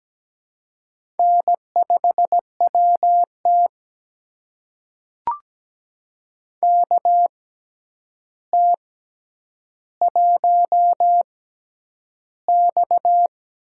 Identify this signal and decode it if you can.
Morse code, heard as a keyed tone, about 17 wpm, spaced as for 9 wpm, 700 Hz. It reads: N5WT KT1X
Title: USA Call Signs - 4 Characters - 6x Character Spacing - 17wpm